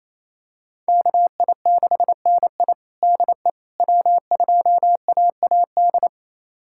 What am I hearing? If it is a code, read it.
Morse code, heard as a keyed tone, about 28 wpm, 700 Hz. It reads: KI6NI DE W2AAD
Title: USA Callsign DE Callsign - 5 Characters - 28wpm